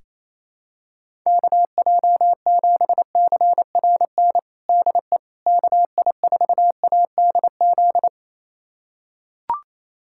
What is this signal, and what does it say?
Morse code, heard as a keyed tone, about 28 wpm, 700 Hz. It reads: KJ7CRN DE KI4ADZ
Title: USA Callsign DE Callsign - 6 Characters - 28wpm